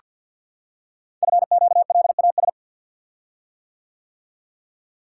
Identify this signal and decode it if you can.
Morse code, heard as a keyed tone, about 46 wpm, 700 Hz. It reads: FQLAS
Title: Code Groups - 5 Letters - 50wpm